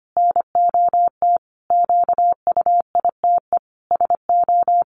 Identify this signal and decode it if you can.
Morse code, heard as a keyed tone, about 25 wpm, 700 Hz. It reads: NOT QUITE SO
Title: Sentences from Top 700 Words 25wpm